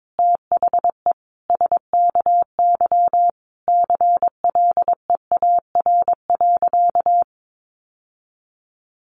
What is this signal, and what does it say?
Morse code, heard as a keyed tone, about 22 wpm, 700 Hz. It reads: THE SKY CLEAR.
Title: Sentences from Top 1200 Words 22wpm